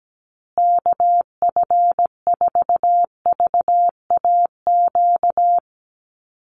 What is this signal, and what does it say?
Morse code, heard as a keyed tone, about 17 wpm, 700 Hz. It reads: KF4VAQ
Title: USA Call Signs Repeated 3x 17wpm